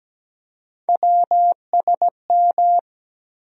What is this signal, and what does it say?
Morse code, heard as a keyed tone, about 17 wpm, 700 Hz. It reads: WSM
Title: USA Call Sign Postfixes - 3 Letters 17wpm